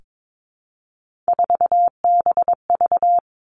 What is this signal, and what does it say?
Morse code, heard as a keyed tone, about 22 wpm, 700 Hz. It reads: 4BV
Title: Three Challenging Letter-Number Combinations 22wpm